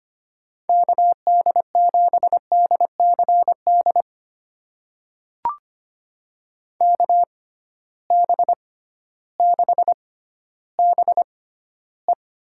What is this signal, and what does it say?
Morse code, heard as a keyed tone, about 25 wpm, spaced as for 13 wpm, 700 Hz. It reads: KD7DCD KB6BE
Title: USA Call Signs - 6 Characters - 6x Character Spacing - 25wpm